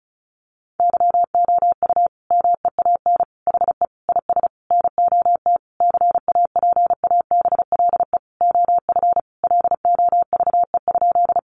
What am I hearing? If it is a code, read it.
Morse code, heard as a keyed tone, about 34 wpm, 700 Hz. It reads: YOU MEAN HE IS NOT CAPABLE OF LOVE?
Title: Sentences from Top 1200 Words 35wpm